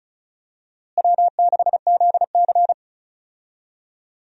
Morse code, heard as a keyed tone, about 36 wpm, 700 Hz. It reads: W6ZC
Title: USA Call Signs - 4 Characters - Repeated 2x - 35wpm